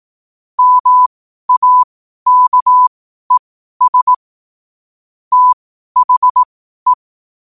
Morse code, heard as a keyed tone, about 18 wpm, spaced as for 12 wpm, 1000 Hz. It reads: MAKES THE
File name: qotdfiles / 12wpm